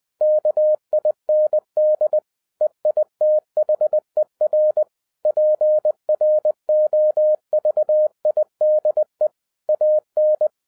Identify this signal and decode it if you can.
Morse code, heard as a keyed tone, about 20 wpm, 610 Hz. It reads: KIND EITHER PROVIDE AN